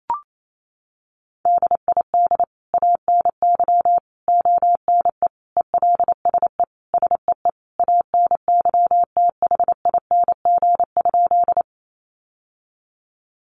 Morse code, heard as a keyed tone, about 28 wpm, 700 Hz. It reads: DID ANY ONE ELSE SEE ANYTHING?
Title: Sentences from Top 1000 Words 28wpm